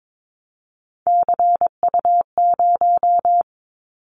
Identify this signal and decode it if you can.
Morse code, heard as a keyed tone, about 22 wpm, 700 Hz. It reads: CU0
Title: Callsign Prefixes for Top 50 DXCC - 22wpm